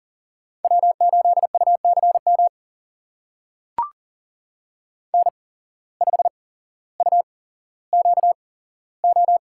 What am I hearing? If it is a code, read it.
Morse code, heard as a keyed tone, about 40 wpm, spaced as for 17 wpm, 700 Hz. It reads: W8UCM N5UQO
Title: USA Call Signs - 5 Characters - 8x Character Spacing - 40wpm